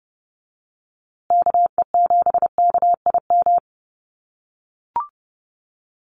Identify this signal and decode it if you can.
Morse code, heard as a keyed tone, about 30 wpm, 700 Hz. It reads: KE7KIM
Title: USA Call Signs Repeated 3x 30wpm